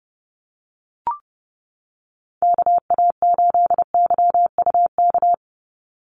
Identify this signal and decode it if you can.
Morse code, heard as a keyed tone, about 30 wpm, 700 Hz. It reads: KA8YUK
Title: USA Call Signs 30wpm